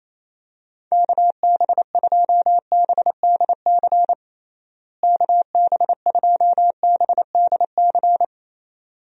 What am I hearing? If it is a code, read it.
Morse code, heard as a keyed tone, about 28 wpm, 700 Hz. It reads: KB2BDC KB2BDC
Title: USA Call Signs - 6 Characters - Repeated 2x - 28wpm